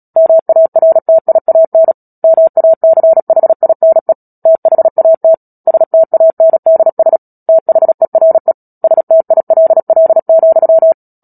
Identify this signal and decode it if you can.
Morse code, heard as a keyed tone, about 37 wpm, 650 Hz. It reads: MARTIAN MACHINE THAT STANDS THERE STILL,
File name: Book Two / 10-Chapter 10 - Epilogue - 36 WPM